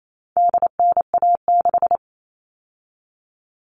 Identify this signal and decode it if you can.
Morse code, heard as a keyed tone, about 28 wpm, 700 Hz. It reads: DNA6
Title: All Cut Numbers - 4-Digit 28wpm